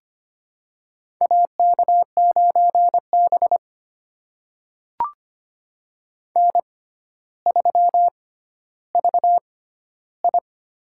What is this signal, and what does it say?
Morse code, heard as a keyed tone, about 25 wpm, spaced as for 13 wpm, 700 Hz. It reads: AK9B N3VI
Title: USA Call Signs - 4 Characters - 6x Character Spacing - 25wpm